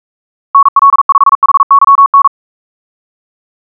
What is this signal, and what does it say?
Morse code, heard as a keyed tone, about 42 wpm, 1110 Hz. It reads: NL5HYN
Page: SampleHST1.wav <-- 158cpm (=47wpm(paris)) 1111Hz sinusoidal waveform